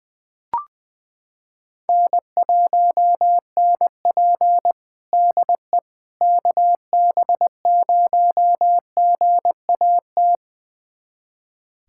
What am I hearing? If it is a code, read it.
Morse code, heard as a keyed tone, about 21 wpm, 700 Hz. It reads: N1NP DE KB0GAT
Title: USA Callsign DE Callsign 20wpm